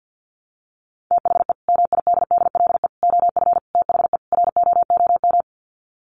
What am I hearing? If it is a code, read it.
Morse code, heard as a keyed tone, about 50 wpm, 700 Hz. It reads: THE MIDDLE OF THE ROOM
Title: Sets of 5 Words Encore2 50wpm